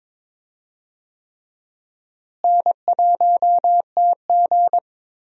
Morse code, heard as a keyed tone, about 23 wpm, 700 Hz. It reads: N1TG
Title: USA Call Signs - 4 Characters - Repeated 3x - 22wpm